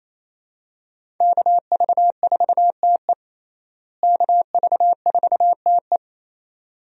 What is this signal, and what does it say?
Morse code, heard as a keyed tone, about 28 wpm, 700 Hz. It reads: KV4TE KV4TE
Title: USA Call Signs - 5 Characters - Repeated 2x - 28wpm